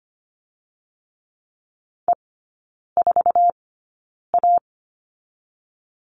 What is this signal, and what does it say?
Morse code, heard as a keyed tone, about 25 wpm, 700 Hz. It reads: E 4 A
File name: Lesson 17e - Letter u - Three Characters 25wpm